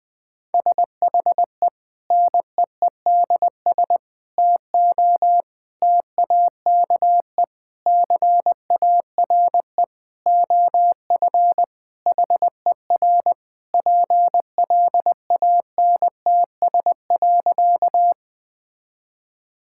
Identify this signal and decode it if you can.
Morse code, heard as a keyed tone, about 21 wpm, 700 Hz. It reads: SHE NEEDS TO TAKE CARE OF HER PLANTS.
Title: Sentences with 4-Word Phrases - 20wpm